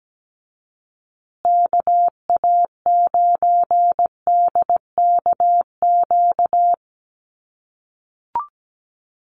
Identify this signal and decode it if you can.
Morse code, heard as a keyed tone, about 17 wpm, 700 Hz. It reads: KA9DKQ
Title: USA Call Signs Repeated 3x 17wpm